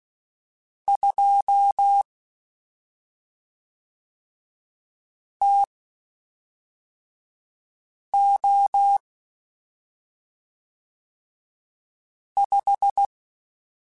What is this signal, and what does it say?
Morse code, heard as a keyed tone, about 16 wpm, spaced as for 3 wpm, 800 Hz. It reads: 2 TO 5
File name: Spaced Characters / Spaced Characters-mp3s / Weather5 12 sp @5